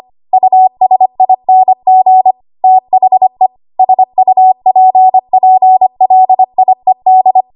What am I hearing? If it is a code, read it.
Morse code, heard as a keyed tone, about 25 wpm, 750 Hz. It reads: USING THE SUPPLIED
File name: Archive / 25 WPM / 25WPM